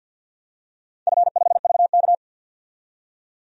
Morse code, heard as a keyed tone, about 51 wpm, 700 Hz. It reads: F5VK
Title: DXCC-100 Call Signs - 4 Characters - 50wpm